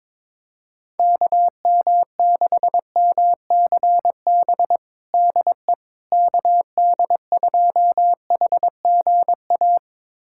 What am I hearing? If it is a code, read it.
Morse code, heard as a keyed tone, about 22 wpm, 700 Hz. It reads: KM6MCB DE KD2HGA
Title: USA Callsign DE Callsign 22wpm